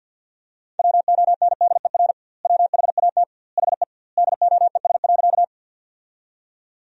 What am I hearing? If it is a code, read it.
Morse code, heard as a keyed tone, about 50 wpm, 700 Hz. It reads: WONDER WHAT HE DOES.
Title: 2x Word-Spacing - Sentences from Top 500 Words - 50wpm